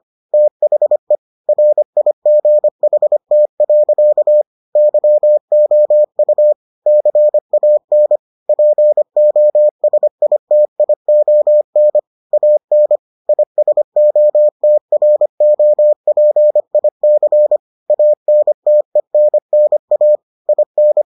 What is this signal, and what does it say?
Morse code, heard as a keyed tone, about 25 wpm, 600 Hz. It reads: THE RIGHT. YOU CAN POSITION AN ISOTROPIC ANTENNA IN